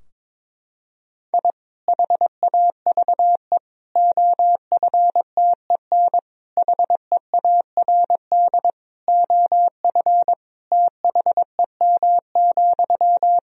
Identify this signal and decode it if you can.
Morse code, heard as a keyed tone, about 22 wpm, 700 Hz. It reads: I HAVE OFTEN HEARD OF THEM,
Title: Sentences from Top 1000 Words - Review Part 2 - 22wpm